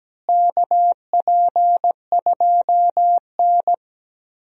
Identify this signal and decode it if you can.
Morse code, heard as a keyed tone, about 17 wpm, 700 Hz. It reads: KP2N
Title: DXCC-100 Call Signs - 4 Characters - Repeated 3x - 17wpm